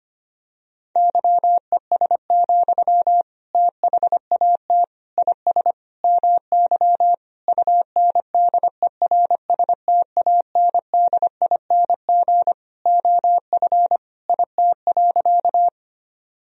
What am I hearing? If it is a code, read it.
Morse code, heard as a keyed tone, about 25 wpm, 700 Hz. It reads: YES, THAT IS MY UNDERSTANDING OF IT.
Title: Sentences from Top 1000 Words - Review Part 3 - 25wpm